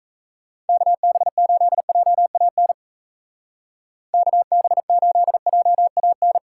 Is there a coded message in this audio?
Yes: Morse code from 0.7 to 6.4 s, heard as a keyed tone, about 35 wpm, 700 Hz. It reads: KB8JAN KB8JAN